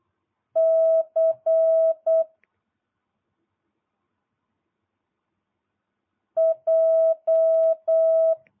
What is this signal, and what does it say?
Morse code, heard as a keyed tone, about 8 wpm, 650 Hz. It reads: C J